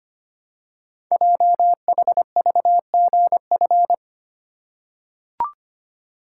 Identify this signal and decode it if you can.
Morse code, heard as a keyed tone, about 25 wpm, 700 Hz. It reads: JHVGF
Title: Code Groups - 5 Letters - 25wpm